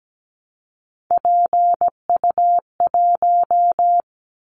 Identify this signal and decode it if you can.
Morse code, heard as a keyed tone, about 17 wpm, 700 Hz. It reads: PU1
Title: Callsign Prefixes for Top 50 DXCC - 17wpm